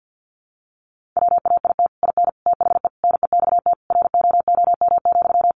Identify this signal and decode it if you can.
Morse code, heard as a keyed tone, about 50 wpm, 700 Hz. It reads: WAIT IN THE NEXT ROOM,
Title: Sentences from Top 1500 Words 50wpm